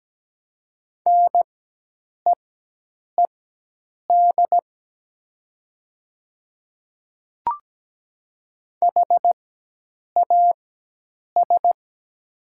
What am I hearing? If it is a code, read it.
Morse code, heard as a keyed tone, about 17 wpm, spaced as for 8 wpm, 700 Hz. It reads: NEED HAS
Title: 4x Character Spacing - Top 200 Words - 17wpm